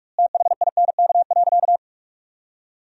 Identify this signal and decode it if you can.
Morse code, heard as a keyed tone, about 46 wpm, 700 Hz. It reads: THINK.
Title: 2x Word-Spacing - Sentences from Top 400 Words - 45wpm